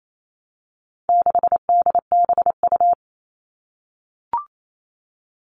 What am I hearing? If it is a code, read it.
Morse code, heard as a keyed tone, about 28 wpm, 700 Hz. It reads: 6DBU
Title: All Cut Numbers - 4-Digit 28wpm